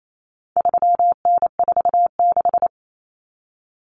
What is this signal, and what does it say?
Morse code, heard as a keyed tone, about 28 wpm, 700 Hz. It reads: 3N46
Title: Common Cut Numbers - 4-Digit 28wpm